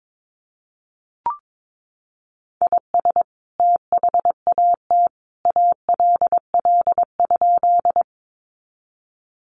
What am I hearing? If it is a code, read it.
Morse code, heard as a keyed tone, about 22 wpm, 700 Hz. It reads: IS THAT ALL?
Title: Sentences from Top 100 Words 22wpm